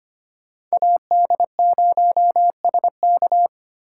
Morse code, heard as a keyed tone, about 25 wpm, 700 Hz. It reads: AD0SK
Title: USA Call Signs - 5 Characters - 2x Character Spacing - 25wpm